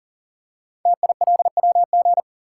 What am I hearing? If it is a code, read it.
Morse code, heard as a keyed tone, about 42 wpm, 690 Hz. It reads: TILWG
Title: Code Groups - 5 Letters - 40wpm